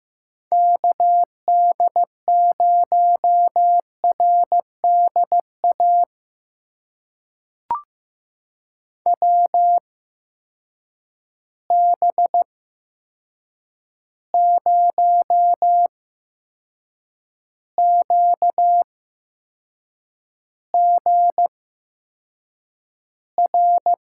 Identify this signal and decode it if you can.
Morse code, heard as a keyed tone, about 15 wpm, spaced as for 6 wpm, 700 Hz. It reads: KD0RDA WB0QGR
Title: USA Call Signs - 6 Characters - 8x Character Spacing - 15wpm